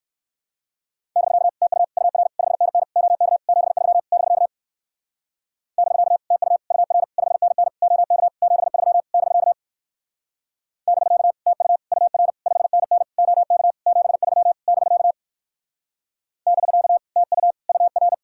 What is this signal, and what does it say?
Morse code, heard as a keyed tone, about 61 wpm, 700 Hz. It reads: <BK> TU UR 5NN OK 73 <BK> <BK> TU UR 5NN OK 73 <BK> <BK> TU UR 5NN OK 73 <BK> <BK> TU UR